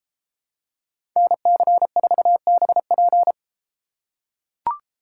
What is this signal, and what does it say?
Morse code, heard as a keyed tone, about 32 wpm, 700 Hz. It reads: NC4BP